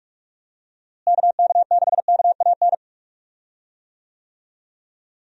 Morse code, heard as a keyed tone, about 41 wpm, 700 Hz. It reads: KK6KAN
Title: USA Call Signs - 6 Characters - Repeated 2x - 45wpm